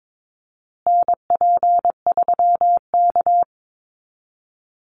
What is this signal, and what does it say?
Morse code, heard as a keyed tone, about 22 wpm, 700 Hz. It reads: NP3K
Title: Call Signs Repeated 3x - 22wpm